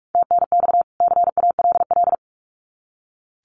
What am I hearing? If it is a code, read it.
Morse code, heard as a keyed tone, about 44 wpm, 700 Hz. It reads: TNX CALL
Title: Common QSO Phrases 45wpm